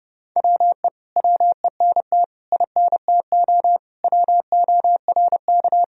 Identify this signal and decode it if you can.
Morse code, heard as a keyed tone, about 30 wpm, 700 Hz. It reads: WE WENT INTO WORK
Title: Sentences from Top 1000 Words - Review Part 4 - 30wpm